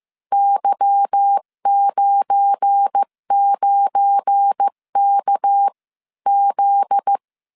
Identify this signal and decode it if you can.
Morse code, heard as a keyed tone, about 15 wpm, spaced as for 13 wpm, 800 Hz. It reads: Y99KZ
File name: CW / DET / 1 9 1500C S B 14wpm 3302